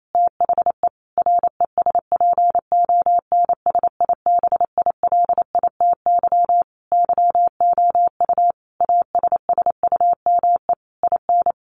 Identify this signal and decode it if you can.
Morse code, heard as a keyed tone, about 28 wpm, 700 Hz. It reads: THE RESPONSIBILITY YOU ASSUME IN
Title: Sentences from Top 1500 Words 28wpm